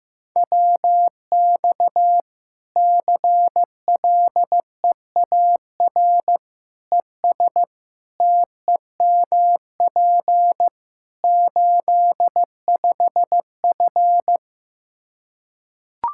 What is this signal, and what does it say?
Morse code, heard as a keyed tone, about 15 wpm, 700 Hz. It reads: WX CLEAR ES TEMP 85F
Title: Practice QSO - Exchange 2 - Info - 15wpm